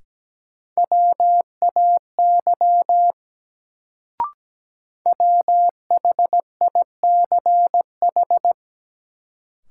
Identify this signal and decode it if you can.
Morse code, heard as a keyed tone, about 17 wpm, 700 Hz. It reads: WAY WHICH
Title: Sentences from Top 100 Words - New Word Drill - 17wpm